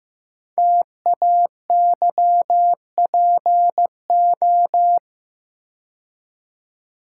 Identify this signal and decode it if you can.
Morse code, heard as a keyed tone, about 15 wpm, 700 Hz. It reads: TAYPO
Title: Code Groups - 5 Letters - 15wpm